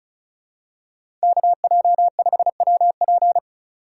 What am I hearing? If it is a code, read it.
Morse code, heard as a keyed tone, about 36 wpm, 700 Hz. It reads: KJ5WP
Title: USA Call Signs - 5 Characters - 2x Character Spacing - 35wpm